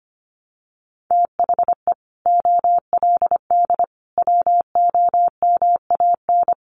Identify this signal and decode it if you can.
Morse code, heard as a keyed tone, about 25 wpm, 700 Hz. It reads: THE OLD WOMAN
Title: Sentences from Top 1100 Words 25wpm